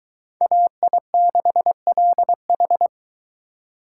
Morse code, heard as a keyed tone, about 22 wpm, 700 Hz. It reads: AI6LH